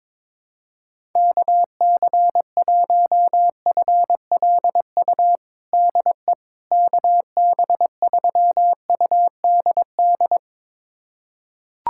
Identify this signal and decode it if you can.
Morse code, heard as a keyed tone, about 22 wpm, 700 Hz. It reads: KC1FLU DE KB3UDD
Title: USA Callsign DE Callsign - 6 Characters - 22wpm